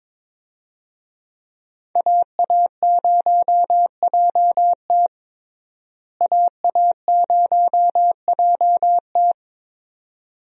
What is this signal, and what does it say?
Morse code, heard as a keyed tone, about 22 wpm, 700 Hz. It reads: AA0JT AA0JT
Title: USA Call Signs - 5 Characters - Repeated 2x - 22wpm